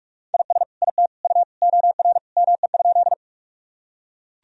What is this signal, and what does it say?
Morse code, heard as a keyed tone, about 42 wpm, 700 Hz. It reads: IS IT U OR ME?
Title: Sentences from Top 200 Words and Abbreviations 45wpm